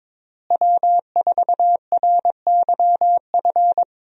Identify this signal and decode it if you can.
Morse code, heard as a keyed tone, about 21 wpm, 700 Hz. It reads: W4RYF
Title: USA Call Signs - 5 Characters - Repeated 2x - 22wpm